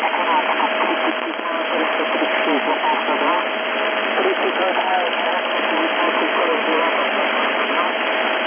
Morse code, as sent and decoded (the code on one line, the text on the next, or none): none